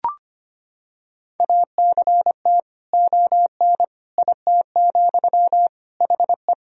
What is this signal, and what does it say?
Morse code, heard as a keyed tone, about 25 wpm, 700 Hz. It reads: ACT ON IT, HE
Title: Sentences from Top 1000 Words - Review Part 1 - 25wpm